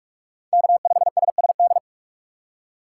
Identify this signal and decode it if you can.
Morse code, heard as a keyed tone, about 45 wpm, 700 Hz. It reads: K5SSD